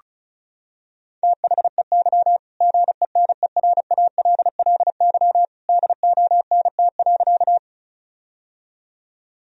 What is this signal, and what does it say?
Morse code, heard as a keyed tone, about 36 wpm, 700 Hz. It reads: THEY GENERALLY DONT.
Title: Sentences from Top 1000 Words - Review Part 4 - 35wpm